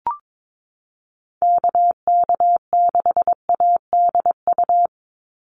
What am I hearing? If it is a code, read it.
Morse code, heard as a keyed tone, about 22 wpm, 700 Hz. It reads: KK6ADU
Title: USA Call Signs 22wpm